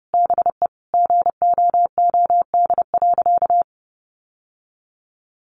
Morse code, heard as a keyed tone, about 30 wpm, 700 Hz. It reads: BE GOOD.
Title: Sentences from Top 1500 Words 30wpm